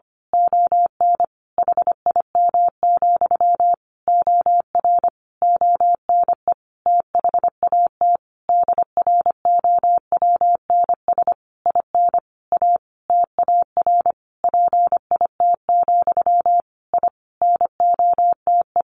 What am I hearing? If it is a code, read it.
Morse code, heard as a keyed tone, about 25 wpm, 700 Hz. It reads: ON HIM, OR ONE THAT DROWNS IN A TAR PIT, I NOTE
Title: A Sound Of Thunder 25wpm